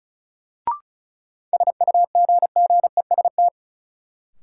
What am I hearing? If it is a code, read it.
Morse code, heard as a keyed tone, about 36 wpm, 700 Hz. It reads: SUGGEST
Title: Sentences from Top 1300 Words - New Word Drill Alt - 35wpm